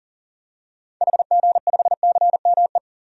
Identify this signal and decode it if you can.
Morse code, heard as a keyed tone, about 37 wpm, 700 Hz. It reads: HG5CME
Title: DXCC-100 Call Signs - 6 Characters - Repeated 3x - 40wpm